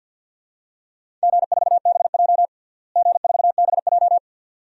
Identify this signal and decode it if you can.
Morse code, heard as a keyed tone, about 51 wpm, 700 Hz. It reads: G4BJ G4BJ
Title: DXCC-100 Call Signs - 4 Characters - Repeated 2x - 50wpm